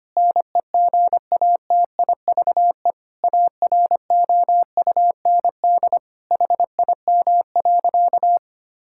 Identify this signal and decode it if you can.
Morse code, heard as a keyed tone, about 25 wpm, 700 Hz. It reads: NEGATIVE AROUND HIM.
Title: Sentences from Top 1000 Words - Review Part 6 - 25wpm